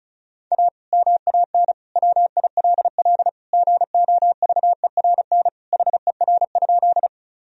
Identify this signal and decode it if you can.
Morse code, heard as a keyed tone, about 34 wpm, 700 Hz. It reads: A MAN WILL GOVERN HER?
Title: Sentences from Top 1000 Words - Review Part 2 - 35wpm